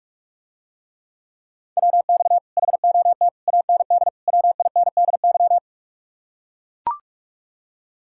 Morse code, heard as a keyed tone, about 44 wpm, 700 Hz. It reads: WX HOT AND WINDY